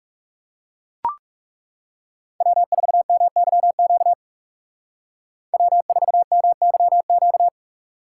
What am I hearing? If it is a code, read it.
Morse code, heard as a keyed tone, about 37 wpm, 700 Hz. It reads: W4MYQ W4MYQ
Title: Speed Racing - USA 1x3 Call Signs 30wpm